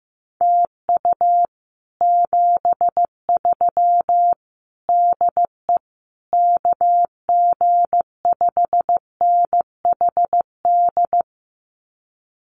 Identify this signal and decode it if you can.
Morse code, heard as a keyed tone, about 15 wpm, 700 Hz. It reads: TU 73 DE KG5NHD
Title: Common QSO Phrases Encore 15wpm